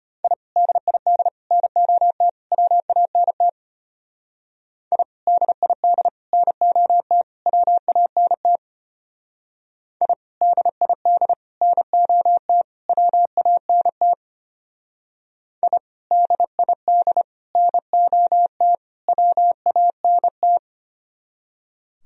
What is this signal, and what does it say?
Morse code, heard as a keyed tone, about 32 wpm, 700 Hz. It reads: I DID NOT WANT I DID NOT WANT I DID NOT WANT I DID NOT WANT